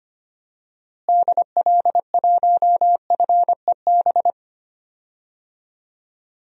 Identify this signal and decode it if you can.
Morse code, heard as a keyed tone, about 25 wpm, 700 Hz. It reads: DL1FEB
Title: DXCC-100 Call Signs - 6 Characters - Repeated 3x - 25wpm